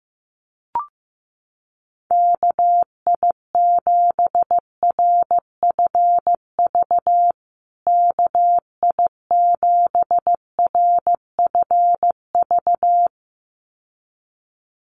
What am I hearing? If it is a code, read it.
Morse code, heard as a keyed tone, about 15 wpm, 700 Hz. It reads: KI7RFV KI7RFV
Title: USA Call Signs Repeated 2x 15wpm